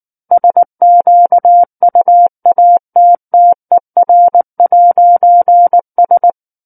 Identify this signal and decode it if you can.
Morse code, heard as a keyed tone, about 19 wpm, 700 Hz. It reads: SQUATTER'S
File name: Book One / 10-Chapter 10 - 19 WPM